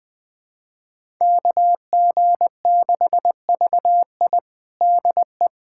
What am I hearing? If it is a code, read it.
Morse code, heard as a keyed tone, about 20 wpm, 700 Hz. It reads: KG6VI DE
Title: USA Callsign DE Callsign - 5 Characters - 20wpm